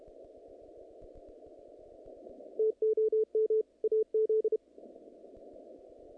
Morse code, heard as a keyed tone, about 31 wpm, 440 Hz. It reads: TOM AZ
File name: contests / naqp-cw-k5tr-op-k5pi- -2200